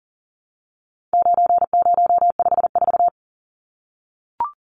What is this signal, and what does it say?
Morse code, heard as a keyed tone, about 40 wpm, 700 Hz. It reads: 9054
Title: Numbers - 4-Digit 40wpm